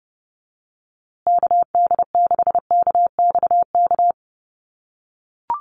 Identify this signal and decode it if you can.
Morse code, heard as a keyed tone, about 30 wpm, 700 Hz. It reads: KD6KXK
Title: USA Call Signs Repeated 2x 30wpm